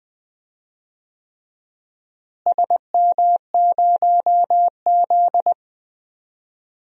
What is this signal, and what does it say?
Morse code, heard as a keyed tone, about 20 wpm, 700 Hz. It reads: SM0Z
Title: DXCC-100 Call Signs - 4 Characters - Repeated 3x - 20wpm